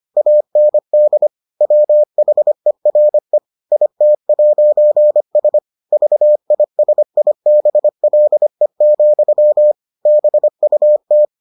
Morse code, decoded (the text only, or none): AND WHERE IT'S VISIBLE, BUT